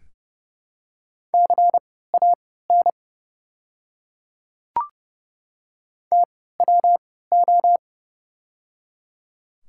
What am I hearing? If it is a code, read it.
Morse code, heard as a keyed tone, about 30 wpm, spaced as for 17 wpm, 700 Hz. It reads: CAN TWO
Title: 3x Character Spacing - Top 100 Words - 30wpm